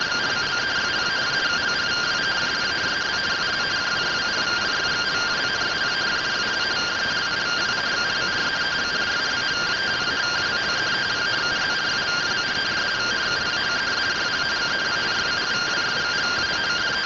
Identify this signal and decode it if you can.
Two-tone teleprinter signal, about 170 Hz shift.